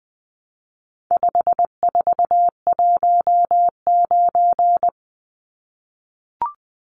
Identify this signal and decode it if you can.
Morse code, heard as a keyed tone, about 20 wpm, 700 Hz. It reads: 5419
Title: Numbers - 4-Digit 20wpm